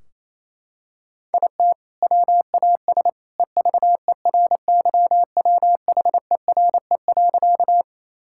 Morse code, heard as a keyed tone, about 28 wpm, 700 Hz. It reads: IT WAS EVERYWHERE.
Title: Sentences from Top 1600 Words 28wpm